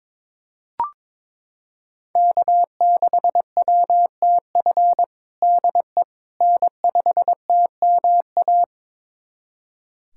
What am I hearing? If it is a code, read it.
Morse code, heard as a keyed tone, about 22 wpm, 700 Hz. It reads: K6WTF DE N5TMA
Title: USA Callsign DE Callsign - 5 Characters - 22wpm